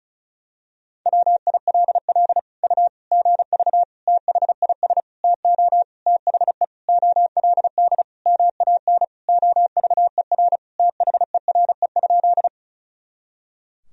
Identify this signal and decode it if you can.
Morse code, heard as a keyed tone, about 36 wpm, 700 Hz. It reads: WILL U GV THIS TO THE OLD MAN OVER THERE?
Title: Sentences from Top 200 Words and Abbreviations 35wpm